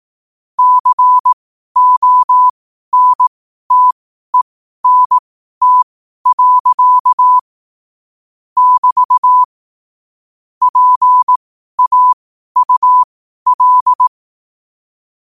Morse code, heard as a keyed tone, about 18 wpm, spaced as for 12 wpm, 1000 Hz. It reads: CONTENT. <BT> PAUL
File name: 12 wpm